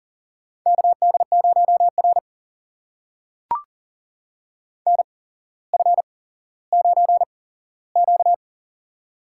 Morse code, heard as a keyed tone, about 38 wpm, spaced as for 17 wpm, 700 Hz. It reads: KD0R NF9Q
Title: USA Call Signs - 4 Characters - 8x Character Spacing - 40wpm